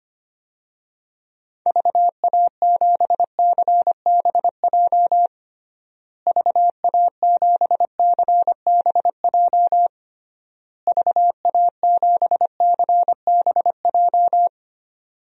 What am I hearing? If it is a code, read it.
Morse code, heard as a keyed tone, about 25 wpm, 700 Hz. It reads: VA7CBJ VA7CBJ VA7CBJ
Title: DXCC-100 Call Signs - 6 Characters - Repeated 3x - 25wpm